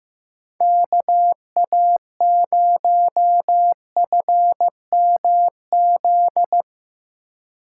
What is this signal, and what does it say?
Morse code, heard as a keyed tone, about 15 wpm, 700 Hz. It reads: KA0FMZ
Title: USA Call Signs - 6 Characters - 2x Character Spacing - 15wpm